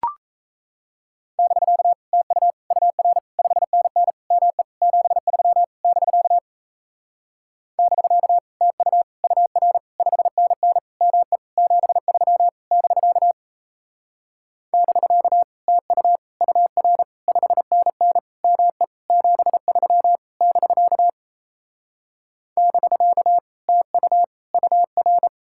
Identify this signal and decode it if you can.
Morse code, heard as a keyed tone, about 36 wpm, 700 Hz. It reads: <BK> TU UR 5NN ME 73 <BK> <BK> TU UR 5NN ME 73 <BK> <BK> TU UR 5NN ME 73 <BK> <BK> TU UR